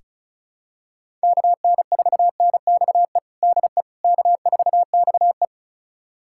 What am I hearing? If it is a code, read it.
Morse code, heard as a keyed tone, about 36 wpm, 700 Hz. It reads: KN4NXE DE K4XE
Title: USA Callsign DE Callsign 35wpm